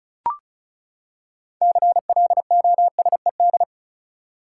Morse code, heard as a keyed tone, about 36 wpm, 700 Hz. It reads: CLOSED
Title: Six Letter Words - 35wpm